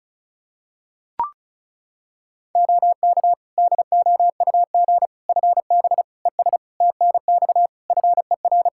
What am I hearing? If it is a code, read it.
Morse code, heard as a keyed tone, about 36 wpm, 700 Hz. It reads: OK DOUG FB ES TNX FER
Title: Practice QSO - Exchange 2 - Info - 35wpm